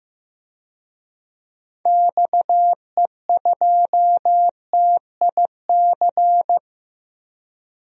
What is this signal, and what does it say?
Morse code, heard as a keyed tone, about 15 wpm, 700 Hz. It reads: XE2TIC
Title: DXCC-100 Call Signs - 6 Characters - 15wpm